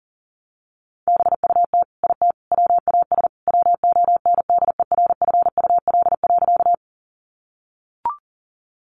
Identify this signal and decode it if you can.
Morse code, heard as a keyed tone, about 40 wpm, 700 Hz. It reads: BUT IT WAS WONDERFUL.
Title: Sentences from Top 1100 Words 40wpm